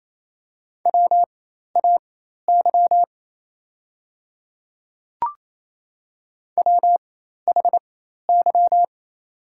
Morse code, heard as a keyed tone, about 28 wpm, spaced as for 13 wpm, 700 Hz. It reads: WAY WHY
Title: 4x Character Spacing - Top 500 Words - 28wpm